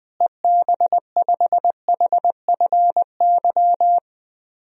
Morse code, heard as a keyed tone, about 20 wpm, 700 Hz. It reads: EB5HFY
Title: DXCC-100 Call Signs - 6 Characters - Repeated 2x - 20wpm